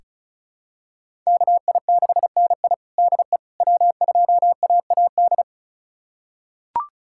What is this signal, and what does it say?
Morse code, heard as a keyed tone, about 36 wpm, 700 Hz. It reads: KI6NI DE W2AAD
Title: USA Callsign DE Callsign - 5 Characters - 35wpm